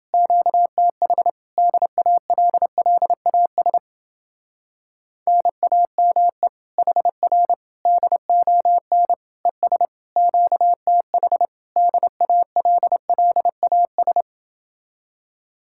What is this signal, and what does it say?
Morse code, heard as a keyed tone, about 29 wpm, 700 Hz. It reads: QTH DALLAS NAME HR DON ES QTH DALLAS